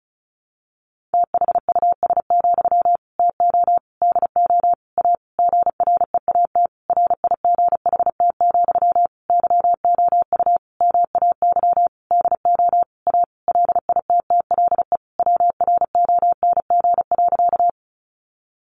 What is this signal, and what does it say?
Morse code, heard as a keyed tone, about 34 wpm, 700 Hz. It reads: THUS, TO DO A GREAT RIGHT, YOU MAY DO A LITTLE WRONG.
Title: Sentences from Top 900 Words 35wpm